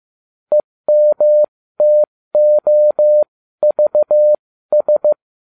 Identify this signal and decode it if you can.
Morse code, heard as a keyed tone, about 15 wpm, spaced as for 13 wpm, 600 Hz. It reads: EMTOVS